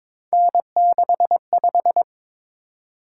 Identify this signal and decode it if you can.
Morse code, heard as a keyed tone, about 22 wpm, 700 Hz. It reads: N65
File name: practice / 22wpm / Three Letter-Number Combinations - 22wpm